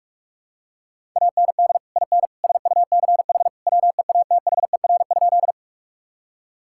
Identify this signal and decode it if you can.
Morse code, heard as a keyed tone, about 46 wpm, 700 Hz. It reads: AND IN SUCH WEATHER?
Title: Sentences from Top 1000 Words - Review Part 5 - 45wpm